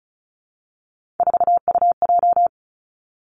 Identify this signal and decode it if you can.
Morse code, heard as a keyed tone, about 36 wpm, 700 Hz. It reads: 4UJ
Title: Three Challenging Letter-Number Combinations 35wpm